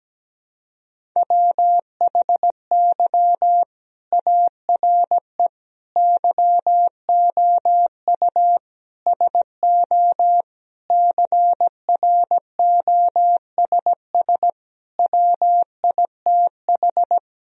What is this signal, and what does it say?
Morse code, heard as a keyed tone, about 17 wpm, 700 Hz. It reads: WHY ARE YOU SO CROSS WITH
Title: Sentences from Top 1000 Words - Review Part 1 - 17wpm